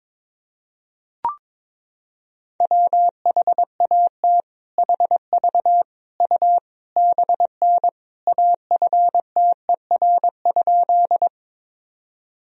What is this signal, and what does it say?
Morse code, heard as a keyed tone, about 22 wpm, 700 Hz. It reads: WHAT HV U BN AFTER?
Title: Sentences from Top 200 Words and Abbreviations 22wpm